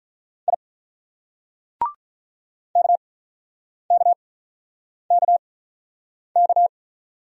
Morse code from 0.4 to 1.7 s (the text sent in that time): I